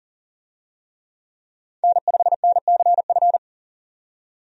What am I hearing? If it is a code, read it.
Morse code, heard as a keyed tone, about 40 wpm, 700 Hz. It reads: N5NCF
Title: USA Call Signs - 5 Characters - 40wpm